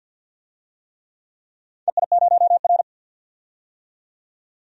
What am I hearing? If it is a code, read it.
Morse code, heard as a keyed tone, about 46 wpm, 700 Hz. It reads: EI0R
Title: DXCC-100 Call Signs - 4 Characters - 50wpm